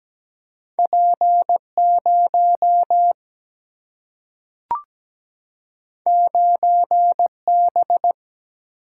Morse code, heard as a keyed tone, about 17 wpm, 700 Hz. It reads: P0 9B
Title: Single-Letter Single-Number 17wpm